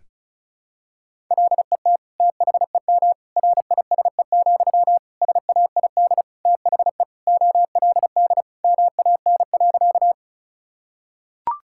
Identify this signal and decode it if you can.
Morse code, heard as a keyed tone, about 36 wpm, 700 Hz. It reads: LET THEM RISE, SAID THE OLD MAN.
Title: Sentences from Top 1000 Words - Review Part 6 - 35wpm